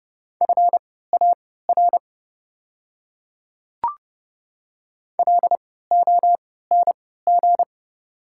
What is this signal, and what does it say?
Morse code, heard as a keyed tone, about 30 wpm, spaced as for 17 wpm, 690 Hz. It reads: FAR LONG
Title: 3x Character Spacing - Top 300 Words - 30wpm